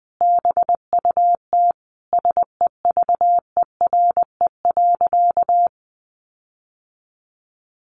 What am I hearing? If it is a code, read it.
Morse code, heard as a keyed tone, about 20 wpm, 700 Hz. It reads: BUT SEVERE.
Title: Sentences from Top 1400 Words 20wpm